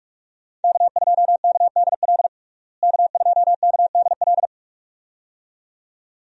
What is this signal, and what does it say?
Morse code, heard as a keyed tone, about 46 wpm, 700 Hz. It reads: K2KDL K2KDL
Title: USA Call Signs - 5 Characters - Repeated 2x - 45wpm